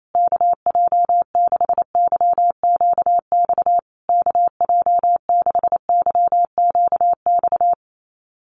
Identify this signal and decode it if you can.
Morse code, heard as a keyed tone, about 28 wpm, 700 Hz. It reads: KJ6YQX KJ6YQX
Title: USA Call Signs Repeated 3x 28wpm